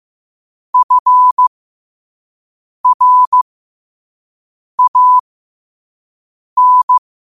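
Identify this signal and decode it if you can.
Morse code, heard as a keyed tone, about 15 wpm, spaced as for 5 wpm, 1000 Hz. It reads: FRAN
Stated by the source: Morse quote of the day at 5 wpm.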